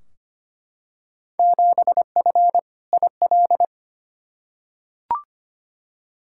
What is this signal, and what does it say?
Morse code, heard as a keyed tone, about 25 wpm, 700 Hz. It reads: 7F IL
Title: Field Day Exchanges Encore - 25wpm